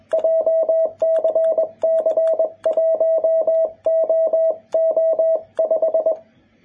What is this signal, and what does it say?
Morse code, heard as a keyed tone, about 21 wpm, 650 Hz. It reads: JCC1OO5
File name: jh8cbh / 2021cw